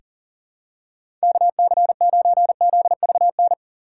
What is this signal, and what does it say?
Morse code, heard as a keyed tone, about 40 wpm, 700 Hz. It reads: KC9ZVN
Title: USA Call Signs - 6x Character Spacing - 40wpm